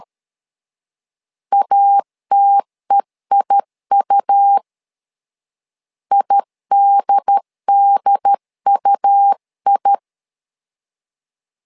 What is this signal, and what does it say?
Morse code, heard as a keyed tone, about 13 wpm, 800 Hz. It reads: ATEIU IDDUI
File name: LEZIONI / 2 UHDN / 2 UHD 12wpm 1540